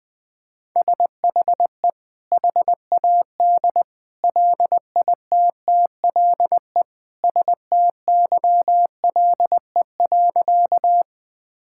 Morse code, heard as a keyed tone, about 20 wpm, 700 Hz. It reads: SHE HAD LITTLE STYLE.
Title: Sentences from Top 1000 Words - Review Part 5 - 20wpm